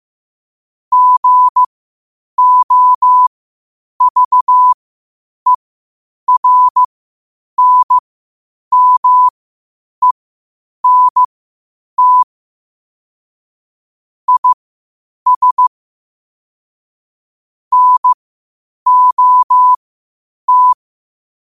Morse code, heard as a keyed tone, about 15 wpm, spaced as for 8 wpm, 1000 Hz. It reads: GOVERNMENT IS NOT
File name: qotdfiles / 8wpm